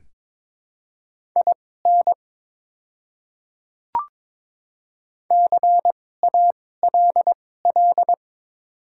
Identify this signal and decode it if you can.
Morse code, heard as a keyed tone, about 22 wpm, spaced as for 16 wpm, 700 Hz. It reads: IN CALL
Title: 2x Character Spacing - Top 100 Words - 22wpm